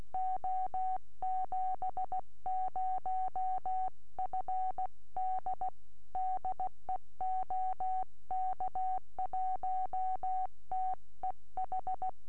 Morse code, heard as a keyed tone, about 16 wpm, 750 Hz. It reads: O70FD DE OK1TEH